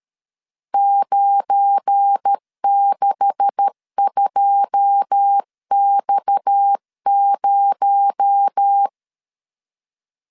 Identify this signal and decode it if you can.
Morse code, heard as a keyed tone, about 13 wpm, 800 Hz. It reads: 962X0
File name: LEZIONI / 9 286 12wpm 2305